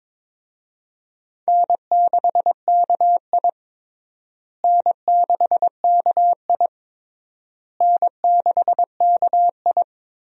Morse code, heard as a keyed tone, about 22 wpm, 700 Hz. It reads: N6KI N6KI N6KI
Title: USA Call Signs - 4 Characters - Repeated 3x - 22wpm